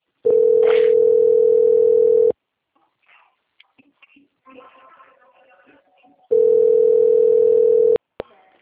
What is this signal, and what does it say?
Ringback tone (North America), 440 + 480 Hz.